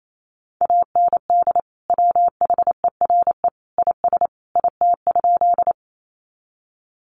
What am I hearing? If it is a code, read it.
Morse code, heard as a keyed tone, about 28 wpm, 700 Hz. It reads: AND WHERE IS IT?
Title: Sentences from Top 200 Words 28wpm